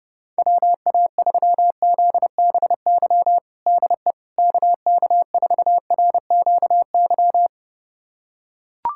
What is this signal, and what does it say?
Morse code, heard as a keyed tone, about 32 wpm, 700 Hz. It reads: WA3ZBY DE KK4RQY
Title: USA Callsign DE Callsign 30wpm